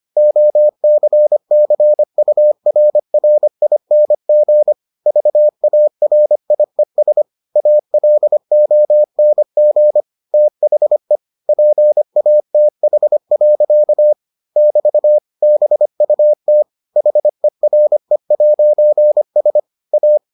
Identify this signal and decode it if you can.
Morse code, heard as a keyed tone, about 25 wpm, 600 Hz. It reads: OCCURRING VARIES ALONG THE PATH. <BT> BUT HERE'S A